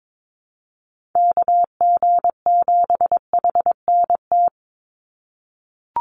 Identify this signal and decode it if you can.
Morse code, heard as a keyed tone, about 22 wpm, 700 Hz. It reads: KG7HNT
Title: USA Call Signs 22wpm